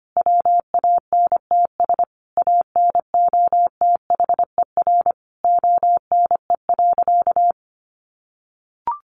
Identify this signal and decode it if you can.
Morse code, heard as a keyed tone, about 25 wpm, 700 Hz. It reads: WANTS ANOTHER ONE.
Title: Easy Phrases 25wpm